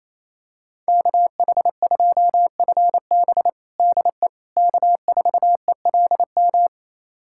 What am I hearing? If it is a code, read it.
Morse code, heard as a keyed tone, about 28 wpm, 700 Hz. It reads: KH2FB DE K4ELM
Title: USA Callsign DE Callsign - 5 Characters - 28wpm